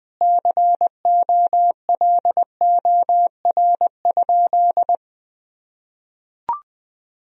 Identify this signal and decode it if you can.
Morse code, heard as a keyed tone, about 21 wpm, 700 Hz. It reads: COLOR?
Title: Sentences from Top 1000 Words - Review Part 2 - 20wpm